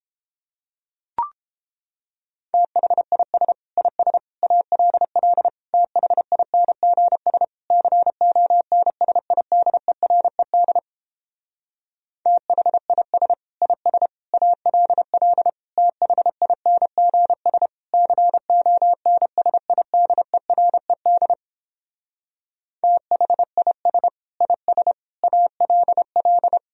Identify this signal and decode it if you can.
Morse code, heard as a keyed tone, about 29 wpm, 700 Hz. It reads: THIS IS ALL THINGS CONSIDERED THIS IS ALL THINGS CONSIDERED THIS IS ALL